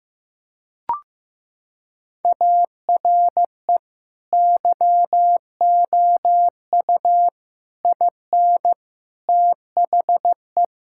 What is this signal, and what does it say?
Morse code, heard as a keyed tone, about 16 wpm, 700 Hz. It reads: ARE YOU IN THE
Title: Sentences from Top 1000 Words - Review Part 6 - 15wpm